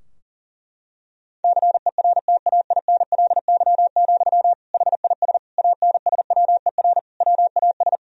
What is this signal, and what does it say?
Morse code, heard as a keyed tone, about 40 wpm, 700 Hz. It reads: CERTAINLY, HIS ANSWER WAS
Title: Sentences from Top 1000 Words - Review Part 2 - 40wpm